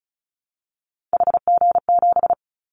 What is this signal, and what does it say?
Morse code, heard as a keyed tone, about 35 wpm, 690 Hz. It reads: HG7
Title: Callsign Prefixes for Top 50 DXCC - 35wpm